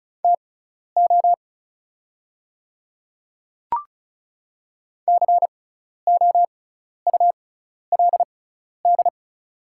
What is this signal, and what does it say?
Morse code, heard as a keyed tone, about 36 wpm, spaced as for 12 wpm, 690 Hz. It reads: TO COULD
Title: 6x Character Spacing - Top 100 Words - 35wpm